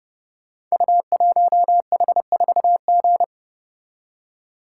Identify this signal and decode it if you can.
Morse code, heard as a keyed tone, about 29 wpm, 700 Hz. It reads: U1H4G
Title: Code Groups - 5 Letter:Numbers - 30wpm